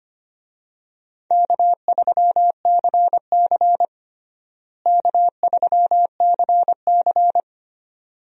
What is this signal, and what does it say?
Morse code, heard as a keyed tone, about 25 wpm, 700 Hz. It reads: K3CC K3CC
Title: USA Call Signs - 4 Characters - Repeated 2x - 25wpm